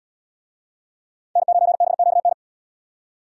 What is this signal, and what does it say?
Morse code, heard as a keyed tone, about 70 wpm, 700 Hz. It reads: N9BQN